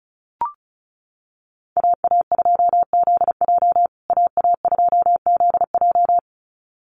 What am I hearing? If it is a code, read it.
Morse code, heard as a keyed tone, about 34 wpm, 700 Hz. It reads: AA2ZJ AA2ZJ
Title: USA Call Signs Repeated 2x 35wpm